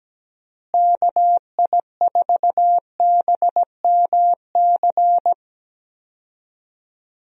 Morse code, heard as a keyed tone, about 17 wpm, 700 Hz. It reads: KI4BMC
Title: USA Call Signs - 6 Characters - Repeated 3x - 17wpm